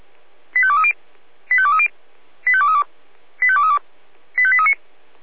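Five-tone selective call (ZVEI), heard as quick runs of five tones, tones of about 80 ms. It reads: 85230 85230 85232 85232 85839
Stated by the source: Received on an SDR.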